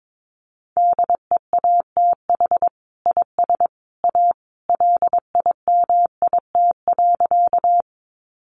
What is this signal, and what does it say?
Morse code, heard as a keyed tone, about 22 wpm, 700 Hz. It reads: DEATH IS A LIMIT.
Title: Sentences from Top 1500 Words 22wpm